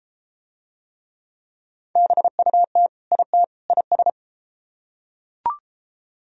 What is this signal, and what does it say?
Morse code, heard as a keyed tone, about 32 wpm, 700 Hz. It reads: BUT IT IS